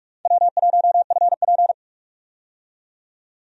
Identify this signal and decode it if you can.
Morse code, heard as a keyed tone, about 46 wpm, 700 Hz. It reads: W1FP
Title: USA Call Signs - 4 Characters - 45wpm